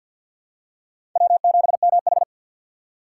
Morse code, heard as a keyed tone, about 53 wpm, 700 Hz. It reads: W7MH
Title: USA Call Signs - 4 Characters - 3x Character Spacing - 50wpm